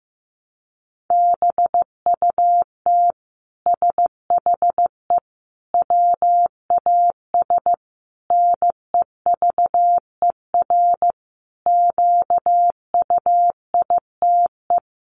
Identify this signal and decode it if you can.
Morse code, heard as a keyed tone, about 15 wpm, 700 Hz. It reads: BUT SHE WAS NEVER QUITE
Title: Sentences from Top 700 Words 15wpm